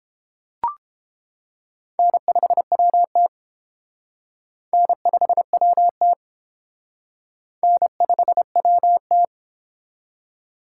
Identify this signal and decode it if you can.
Morse code, heard as a keyed tone, about 30 wpm, 700 Hz. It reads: N5WT N5WT N5WT